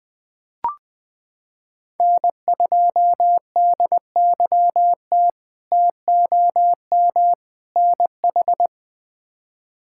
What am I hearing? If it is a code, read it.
Morse code, heard as a keyed tone, about 20 wpm, 700 Hz. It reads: N2DYT TOM NH
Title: K1USN SST QSO Practice - 20wpm